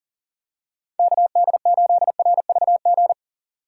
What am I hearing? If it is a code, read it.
Morse code, heard as a keyed tone, about 40 wpm, 700 Hz. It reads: KD8RVG
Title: USA Call Signs - 6 Characters - 3x Character Spacing - 40wpm